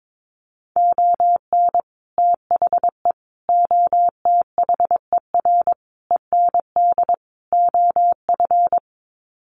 Morse code, heard as a keyed tone, about 22 wpm, 700 Hz. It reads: ON THE OTHER END OF
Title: Sets of 5 Words Encore 22wpm